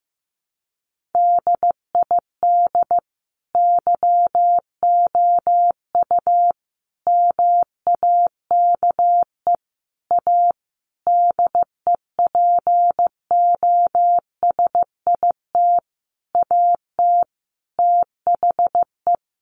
Morse code, heard as a keyed tone, about 15 wpm, 700 Hz. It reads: DID YOU MAKE A DEPOSIT AT THE
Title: Sentences from Top 1300 Words 15wpm